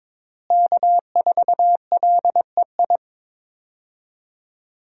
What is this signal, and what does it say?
Morse code, heard as a keyed tone, about 22 wpm, 700 Hz. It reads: K4LEI
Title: USA Call Signs - 5 Characters - Repeated 3x - 22wpm